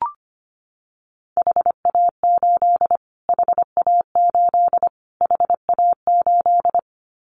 Morse code, heard as a keyed tone, about 25 wpm, 700 Hz. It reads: HA8 HA8 HA8
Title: Callsign Prefixes for Top 25 DXCC - Repeated 3x - 25wpm